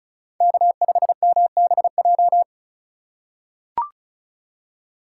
Morse code, heard as a keyed tone, about 36 wpm, 700 Hz. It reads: K5MBJ
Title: USA Call Signs - 5 Characters - 3x Character Spacing - 35wpm